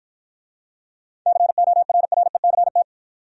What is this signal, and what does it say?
Morse code, heard as a keyed tone, about 48 wpm, 700 Hz. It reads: CORRECT